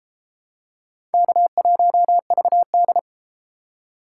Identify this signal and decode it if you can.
Morse code, heard as a keyed tone, about 30 wpm, 700 Hz. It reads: K1VD